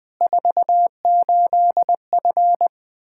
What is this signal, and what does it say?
Morse code, heard as a keyed tone, about 20 wpm, 700 Hz. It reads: 48F
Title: K1USN SST QSO Practice 20wpm